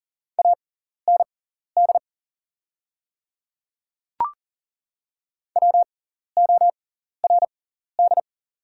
Morse code, heard as a keyed tone, about 40 wpm, spaced as for 14 wpm, 700 Hz. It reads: AND WORD
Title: 6x Character Spacing - Top 100 Words - 40wpm